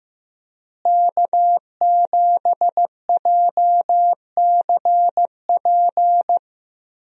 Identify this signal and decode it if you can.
Morse code, heard as a keyed tone, about 15 wpm, 700 Hz. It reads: K7JCP
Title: USA Call Signs - 5 Characters - Repeated 3x - 15wpm